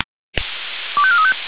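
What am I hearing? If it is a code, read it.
Five-tone selective call (ZVEI), heard as quick runs of five tones, tones of about 80 ms. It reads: 25435
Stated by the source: Received on an SDR.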